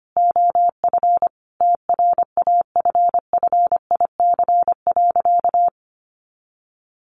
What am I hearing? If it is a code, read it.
Morse code, heard as a keyed tone, about 25 wpm, 700 Hz. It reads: OF TRAFFIC.
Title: Sentences from Top 1300 Words 25wpm